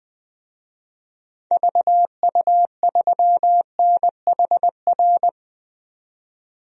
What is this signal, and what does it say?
Morse code, heard as a keyed tone, about 20 wpm, 700 Hz. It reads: VU3NHR
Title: DXCC-100 Call Signs - 6 Characters - 20wpm